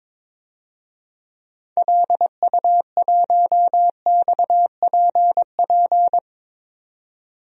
Morse code, heard as a keyed tone, about 22 wpm, 700 Hz. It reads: LU1XPP
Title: DXCC-100 Call Signs - 6 Characters - 22wpm